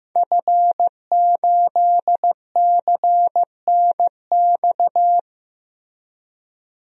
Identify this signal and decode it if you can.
Morse code, heard as a keyed tone, about 15 wpm, 700 Hz. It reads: F8CNX
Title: DXCC-100 Call Signs - 5 Characters - Repeated 2x - 15wpm